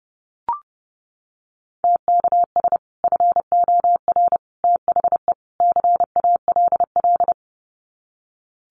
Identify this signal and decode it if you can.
Morse code, heard as a keyed tone, about 30 wpm, 700 Hz. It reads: TKS FOR THE CALL
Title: Common QSO Phrases 30wpm